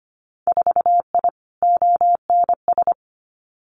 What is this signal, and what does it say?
Morse code, heard as a keyed tone, about 25 wpm, 700 Hz. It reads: 4I ONS
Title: Winter Field Day Exchanges Encore 25wpm